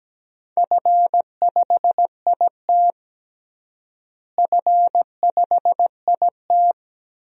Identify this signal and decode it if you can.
Morse code, heard as a keyed tone, about 17 wpm, 700 Hz. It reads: F5IT F5IT
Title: DXCC-100 Call Signs - 4 Characters - Repeated 2x - 17wpm